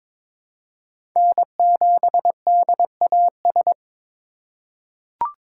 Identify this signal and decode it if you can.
Morse code, heard as a keyed tone, about 22 wpm, 700 Hz. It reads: N7DAS
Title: USA Call Signs - 4x Character Spacing - 22wpm